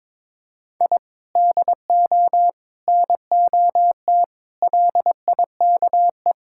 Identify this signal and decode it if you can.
Morse code, heard as a keyed tone, about 22 wpm, 700 Hz. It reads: I DO NOT LIKE
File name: practice / 22wpm / Sentences with 3-Word Phrases - 22wpm